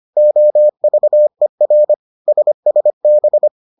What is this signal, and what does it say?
Morse code, heard as a keyed tone, about 25 wpm, 600 Hz. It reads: OVER SSB